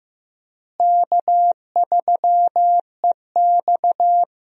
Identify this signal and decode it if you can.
Morse code, heard as a keyed tone, about 15 wpm, 700 Hz. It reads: K3EX
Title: USA Call Signs - 4 Characters - Repeated 3x - 15wpm